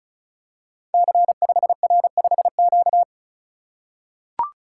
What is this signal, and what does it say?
Morse code, heard as a keyed tone, about 36 wpm, 690 Hz. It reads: C5R5Q
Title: Code Groups - 5 Letter:Numbers - 35wpm